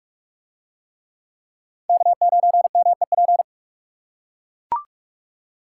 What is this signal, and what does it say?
Morse code, heard as a keyed tone, about 44 wpm, 700 Hz. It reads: K9MEP